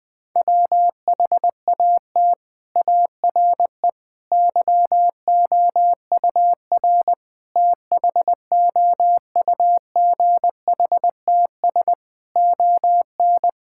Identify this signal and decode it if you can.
Morse code, heard as a keyed tone, about 21 wpm, 700 Hz. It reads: WHAT ARE YOUR THOUGHTS ON
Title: Sentences with 6-Word Phrases - Spoken Beginning - 20wpm